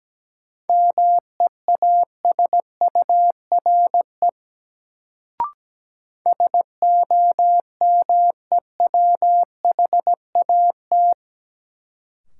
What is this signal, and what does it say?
Morse code, heard as a keyed tone, about 17 wpm, 700 Hz. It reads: MEASURE SOMEWHAT
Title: Sentences from Top 1000 Words - Review Part 1 - New Word Drill - 17wpm